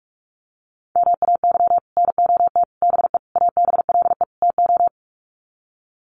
Morse code, heard as a keyed tone, about 44 wpm, 700 Hz. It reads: MAY NOT BE ABLE TO
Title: Sets of 5 Words 45wpm